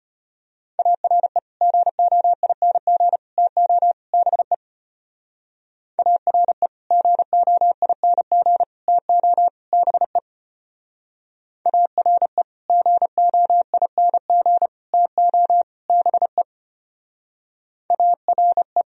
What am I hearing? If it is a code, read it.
Morse code, heard as a keyed tone, about 30 wpm, 700 Hz. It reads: ARE GOING TO BE ARE GOING TO BE ARE GOING TO BE ARE
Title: Speed-Racing - Sets of 4 Words 25wpm